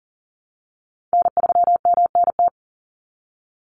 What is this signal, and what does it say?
Morse code, heard as a keyed tone, about 40 wpm, 700 Hz. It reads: N3MNT
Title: Call Signs Repeated 3x - 40wpm